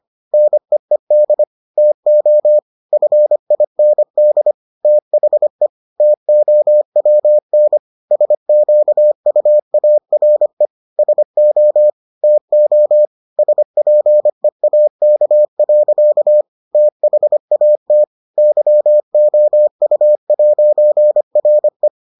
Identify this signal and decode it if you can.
Morse code, heard as a keyed tone, about 25 wpm, 600 Hz. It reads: NEED TO FIND THE TOWN SQUARE SO TO SPEAK. THAT YOU'RE